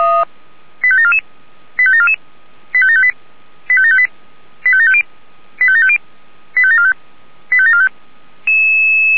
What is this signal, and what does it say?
Five-tone selective call (ZVEI), heard as quick runs of five tones, tones of about 70 ms. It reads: 85744 85744 85758 85758 85750 85750 85745 85745